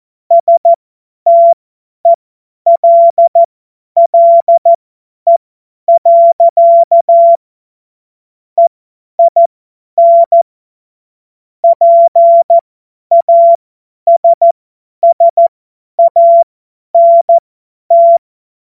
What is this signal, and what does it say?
Morse code, heard as a keyed tone, about 14 wpm, spaced as for 10 wpm, 680 Hz. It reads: STELLE. EIN PASSANT
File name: Morsekurs / Witze / Witz 05 Dezember 2023 50 F70